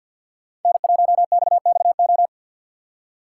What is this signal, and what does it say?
Morse code, heard as a keyed tone, about 49 wpm, 700 Hz. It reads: N1XXO
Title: USA Call Signs - 5 Characters - 6x Character Spacing - 50wpm